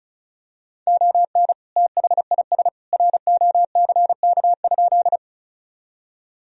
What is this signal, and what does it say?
Morse code, heard as a keyed tone, about 34 wpm, 700 Hz. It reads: ON THIS ROCK?
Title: Sentences from Top 1000 Words - Review Part 2 - 35wpm